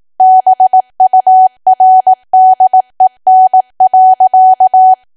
Morse code, heard as a keyed tone, about 18 wpm, 750 Hz. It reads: BURDEN.
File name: cwa / W1AW-Level2 / 18WPM